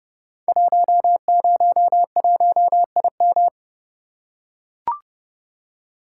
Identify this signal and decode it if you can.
Morse code, heard as a keyed tone, about 30 wpm, 700 Hz. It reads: 101IM
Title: Code Groups - 5 Letter:Numbers - 30wpm